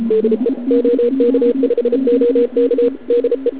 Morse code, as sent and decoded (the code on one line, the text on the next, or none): -.. / --- -.- ..... --- -.- / -.. .
DE OK5OK DE